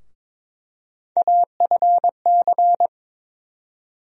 Morse code, heard as a keyed tone, about 22 wpm, 700 Hz. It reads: AFC
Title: USA Call Sign Postfixes - 3 Letters 22wpm